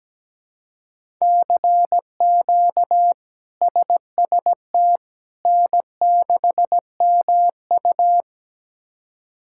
Morse code, heard as a keyed tone, about 17 wpm, 700 Hz. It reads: CQ SST N6MU
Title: K1USN SST QSO Practice - 17wpm